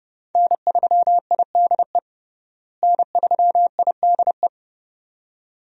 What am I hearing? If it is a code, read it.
Morse code, heard as a keyed tone, about 32 wpm, 700 Hz. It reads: N3IDE N3IDE
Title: USA Call Signs - 5 Characters - Repeated 2x - 30wpm